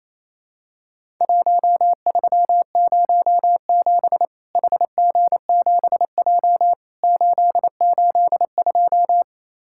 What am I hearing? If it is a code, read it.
Morse code, heard as a keyed tone, about 28 wpm, 700 Hz. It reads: 1307 HG7J 882
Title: Worked All Europe DX Contest - QTC Exchange - 28wpm